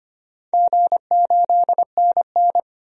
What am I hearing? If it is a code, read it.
Morse code, heard as a keyed tone, about 25 wpm, 700 Hz. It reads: G8NN
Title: DXCC-100 Call Signs - 4 Characters - 25wpm